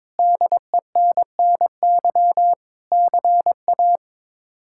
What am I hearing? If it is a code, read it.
Morse code, heard as a keyed tone, about 21 wpm, 700 Hz. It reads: DENNY CA
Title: K1USN SST QSO Practice - 22wpm